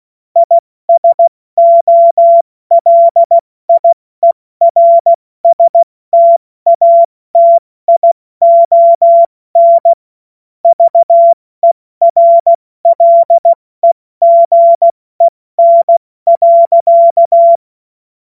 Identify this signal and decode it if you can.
Morse code, heard as a keyed tone, about 16 wpm, 680 Hz. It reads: ISOLIERSTATION VERLEGEN.
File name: Morsekurs / Witze / Witz 27. Juni 2023 70 F80